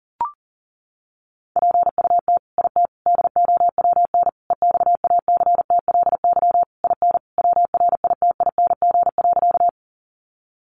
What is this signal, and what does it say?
Morse code, heard as a keyed tone, about 40 wpm, 700 Hz. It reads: PUT IT DOWN EXACTLY IN WRITING.
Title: Sentences from Top 900 Words 40wpm